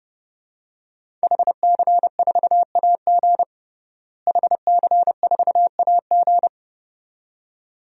Morse code, heard as a keyed tone, about 32 wpm, 700 Hz. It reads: HC4AG HC4AG
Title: DXCC-100 Call Signs - 5 Characters - Repeated 2x - 30wpm